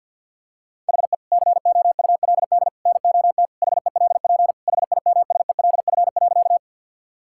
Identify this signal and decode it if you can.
Morse code, heard as a keyed tone, about 50 wpm, 700 Hz. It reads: HE COULD NOT HELP HIMSELF.
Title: Sentences from Top 1600 Words 50wpm